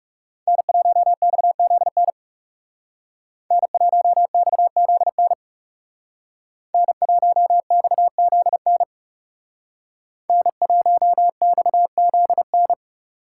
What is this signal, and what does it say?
Morse code, heard as a keyed tone, about 38 wpm, 700 Hz. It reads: N1XZN N1XZN N1XZN N1XZN